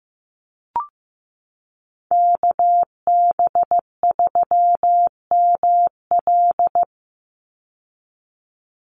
Morse code, heard as a keyed tone, about 15 wpm, 700 Hz. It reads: KB3ML
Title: Call Signs Repeated 3x - 15wpm